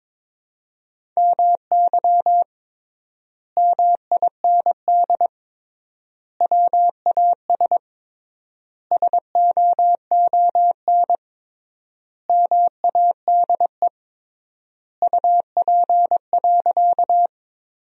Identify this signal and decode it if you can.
Morse code, heard as a keyed tone, about 22 wpm, 700 Hz. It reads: MY MIND WAS SOON MADE UP.
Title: 3x Word-Spacing - Sentences from Top 500 Words - 22wpm